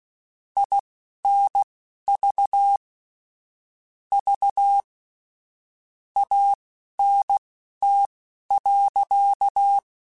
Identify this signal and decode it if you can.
Morse code, heard as a keyed tone, about 16 wpm, spaced as for 11 wpm, 800 Hz. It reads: INV V ANT.
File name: (spaced) / QSOs Spaced-mp3s / WD0FBT 1 @15 sp